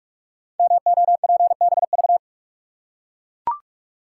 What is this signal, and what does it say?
Morse code, heard as a keyed tone, about 44 wpm, 690 Hz. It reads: MOPBV
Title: Code Groups - 5 Letters - 45wpm